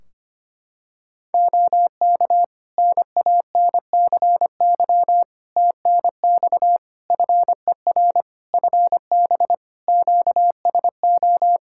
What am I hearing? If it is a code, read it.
Morse code, heard as a keyed tone, about 25 wpm, 700 Hz. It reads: OK NANCY TNX FER FB QSO
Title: Practice QSO - Exchange 4 - Ending - 25wpm